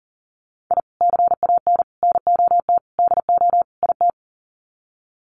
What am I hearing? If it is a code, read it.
Morse code, heard as a keyed tone, about 40 wpm, 700 Hz. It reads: I CAN NOT DO IT
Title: Sets of 5 Words Encore 40wpm